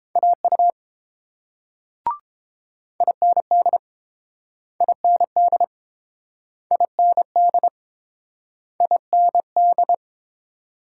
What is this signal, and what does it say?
Morse code, heard as a keyed tone, about 30 wpm, 700 Hz. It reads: AU IND IND IND IND